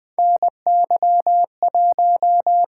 Morse code, heard as a keyed tone, about 20 wpm, 700 Hz. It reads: NY1
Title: USA Call Signs - 4 Characters - 8x Character Spacing - 20wpm